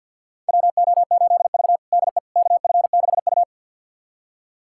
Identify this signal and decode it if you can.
Morse code, heard as a keyed tone, about 53 wpm, 700 Hz. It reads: WO8V DE KF6U
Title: USA Callsign DE Callsign - 4 Characters - 50wpm